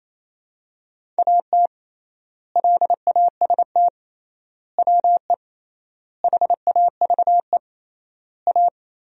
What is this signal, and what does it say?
Morse code, heard as a keyed tone, about 28 wpm, 700 Hz. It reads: AT LAST WE HAVE A
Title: 3x Word-Spacing - Sentences from Top 300 Words - 28wpm